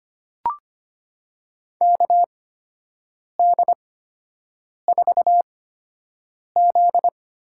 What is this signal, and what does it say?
Morse code, heard as a keyed tone, about 25 wpm, spaced as for 7 wpm, 700 Hz. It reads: KD4Z
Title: USA Call Signs - 6 Characters - 8x Character Spacing - 25wpm